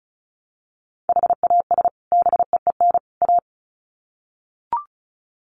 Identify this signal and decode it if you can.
Morse code, heard as a keyed tone, about 34 wpm, 700 Hz. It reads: HAS BEEN A
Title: Sets of 3 Words 35wpm